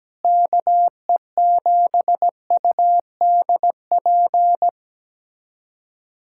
Morse code, heard as a keyed tone, about 17 wpm, 700 Hz. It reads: KE7UDP
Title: USA Callsign DE Callsign - 6 Characters - 17wpm